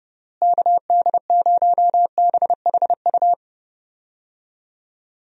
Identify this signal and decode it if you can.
Morse code, heard as a keyed tone, about 32 wpm, 700 Hz. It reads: KD0BHU
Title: USA Call Signs - 6 Characters - Repeated 2x - 30wpm